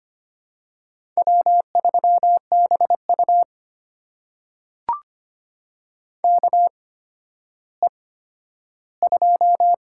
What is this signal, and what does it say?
Morse code, heard as a keyed tone, about 25 wpm, spaced as for 12 wpm, 700 Hz. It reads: W3BU KE2
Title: USA Call Signs - 8x Character Spacing - 25wpm